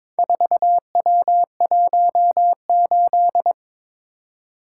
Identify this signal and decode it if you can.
Morse code, heard as a keyed tone, about 22 wpm, 700 Hz. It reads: 4W18
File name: practice / 22wpm / Four Letter-Number Combinations - 22wpm